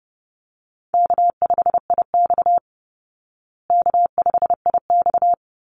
Morse code, heard as a keyed tone, about 30 wpm, 700 Hz. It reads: K5IX K5IX
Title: Call Signs Repeated 3x - 30wpm